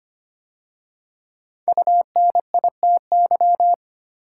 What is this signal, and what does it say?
Morse code, heard as a keyed tone, about 25 wpm, 700 Hz. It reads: UNITY
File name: 5 Letter Words - First 3 Letters Given - 25wpm